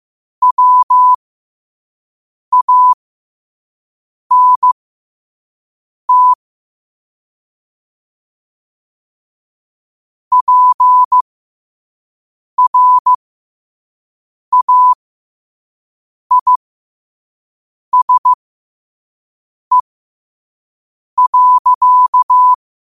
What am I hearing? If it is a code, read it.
Morse code, heard as a keyed tone, about 15 wpm, spaced as for 5 wpm, 1000 Hz. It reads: WANT PRAISE.
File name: qotdfiles / 5wpm